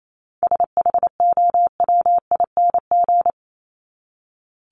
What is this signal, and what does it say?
Morse code, heard as a keyed tone, about 28 wpm, 700 Hz. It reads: SHOWING
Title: ING Suffix - Top 100 Words with Base Words 28wpm